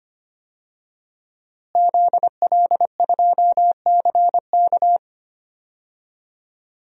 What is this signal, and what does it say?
Morse code, heard as a keyed tone, about 25 wpm, 700 Hz. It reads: ZL2CK
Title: DXCC-100 Call Signs - 5 Characters - Repeated 3x - 25wpm